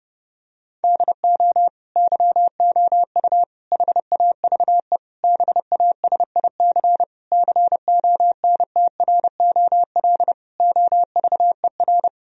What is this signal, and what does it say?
Morse code, heard as a keyed tone, about 32 wpm, 700 Hz. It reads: DO YOU HAVE BASIC CONTROL OVER
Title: Sentences from Top 1000 Words - Review Part 3 - 30wpm